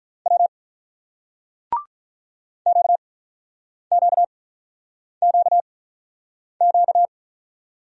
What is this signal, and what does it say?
Morse code, heard as a keyed tone, about 42 wpm, spaced as for 8 wpm, 700 Hz. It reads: W QQQQ
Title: Speed-Racing Single Letters 35wpm